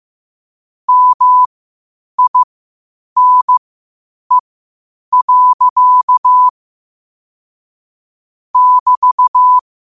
Morse code, heard as a keyed tone, about 15 wpm, spaced as for 8 wpm, 1000 Hz. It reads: MINE. <BT>
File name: qotdfiles / 8wpm